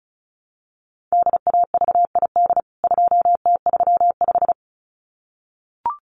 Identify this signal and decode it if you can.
Morse code, heard as a keyed tone, about 34 wpm, 700 Hz. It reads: DAVID 2T35
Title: CWT Exchange 35wpm